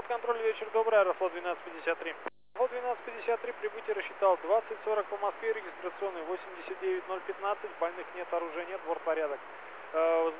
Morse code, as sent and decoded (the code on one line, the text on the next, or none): none